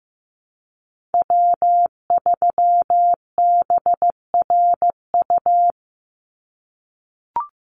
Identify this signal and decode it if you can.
Morse code, heard as a keyed tone, about 15 wpm, 700 Hz. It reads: W3BRU
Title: USA Call Signs 15wpm